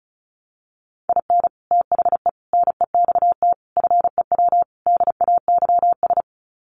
Morse code, heard as a keyed tone, about 34 wpm, 700 Hz. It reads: IN THE NEXT FEW DAYS
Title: Sets of 5 Words Encore2 35wpm